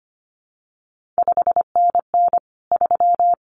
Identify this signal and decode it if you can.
Morse code, heard as a keyed tone, about 25 wpm, 700 Hz. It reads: 5NN 3
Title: CQ WW DX Contest Exchange - 25wpm